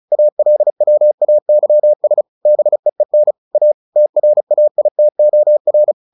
Morse code, heard as a keyed tone, about 34 wpm, 600 Hz. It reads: ALWAYS BEEN A TRAITOR
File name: the old man and the sea 2 35wpm